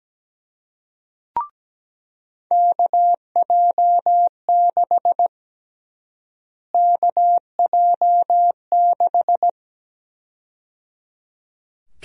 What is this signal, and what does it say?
Morse code, heard as a keyed tone, about 17 wpm, 700 Hz. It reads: KJ6 KJ6
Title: USA Call Sign Prefixes - Repeated 2x - 17wpm